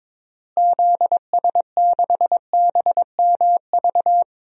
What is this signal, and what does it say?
Morse code, heard as a keyed tone, about 21 wpm, 700 Hz. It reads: ZS6BMV
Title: DXCC-100 Call Signs - 6 Characters - Repeated 3x - 22wpm